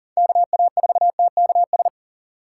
Morse code, heard as a keyed tone, about 40 wpm, 700 Hz. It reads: KA4TKS
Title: USA Call Signs - 6 Characters - 6x Character Spacing - 40wpm